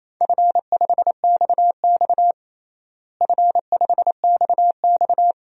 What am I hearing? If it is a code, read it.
Morse code, heard as a keyed tone, about 28 wpm, 700 Hz. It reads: F5XX F5XX
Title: DXCC-100 Call Signs - 4 Characters - Repeated 2x - 28wpm